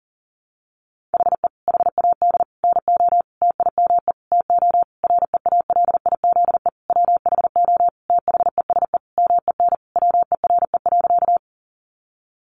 Morse code, heard as a keyed tone, about 40 wpm, 700 Hz. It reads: HE HAD NO TIME TO REALIZE WHO THESE MEN WERE.
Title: Sentences from Top 1000 Words 40wpm